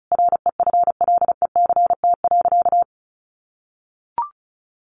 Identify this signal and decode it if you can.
Morse code, heard as a keyed tone, about 34 wpm, 700 Hz. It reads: REFLECT.
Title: Sentences from Top 1500 Words 35wpm